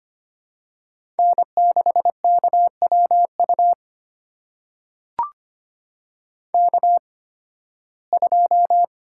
Morse code, heard as a keyed tone, about 25 wpm, spaced as for 16 wpm, 700 Hz. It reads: N6KWU K2
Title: USA Call Signs - 5 Characters - 8x Character Spacing - 25wpm